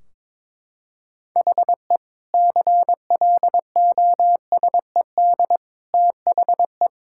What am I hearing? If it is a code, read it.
Morse code, heard as a keyed tone, about 21 wpm, 700 Hz. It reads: HE CLOSED THE
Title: Sentences from Top 1600 Words 22wpm